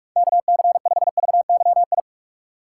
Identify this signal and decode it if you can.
Morse code, heard as a keyed tone, about 46 wpm, 690 Hz. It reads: KC5VYI
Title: USA Call Signs - 3x Character Spacing - 45wpm